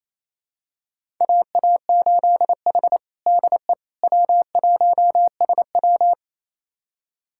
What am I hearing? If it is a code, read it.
Morse code, heard as a keyed tone, about 28 wpm, 700 Hz. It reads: AA8H DE W1SW
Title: USA Callsign DE Callsign - 4 Characters - 28wpm